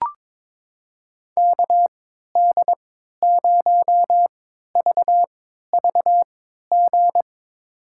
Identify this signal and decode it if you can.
Morse code, heard as a keyed tone, about 23 wpm, spaced as for 13 wpm, 700 Hz. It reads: KD0VVG
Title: USA Call Signs - 6 Characters - 3x Character Spacing - 22wpm